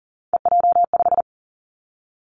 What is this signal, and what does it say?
Morse code, heard as a keyed tone, about 38 wpm, 690 Hz. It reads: EJ5
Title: Three Challenging Letter-Number Combinations 40wpm